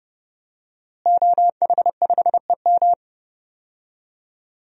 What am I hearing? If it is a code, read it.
Morse code, heard as a keyed tone, about 31 wpm, 700 Hz. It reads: OH5EM
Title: DXCC-100 Call Signs - 5 Characters - Repeated 3x - 30wpm